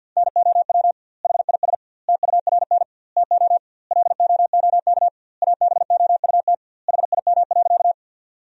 Morse code, heard as a keyed tone, about 52 wpm, 700 Hz. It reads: NOW HIS TURN TO LOOK ABOUT HIM.
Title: 2x Word-Spacing - Sentences from Top 200 Words - 50wpm